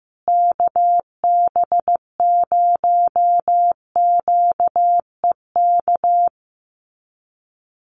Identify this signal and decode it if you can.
Morse code, heard as a keyed tone, about 15 wpm, 700 Hz. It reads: KB0QEK
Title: USA Call Signs Repeated 2x 15wpm